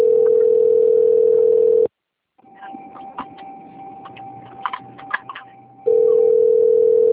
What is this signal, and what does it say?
Ringback tone (North America), 440 + 480 Hz.
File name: rg-1600-unknown- - - .27267